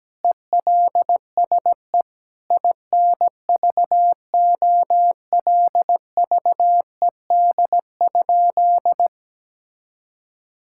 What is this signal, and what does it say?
Morse code, heard as a keyed tone, about 17 wpm, 700 Hz. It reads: ELSE INVOLVED?
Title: Sentences from Top 1000 Words - Review Part 5 - 17wpm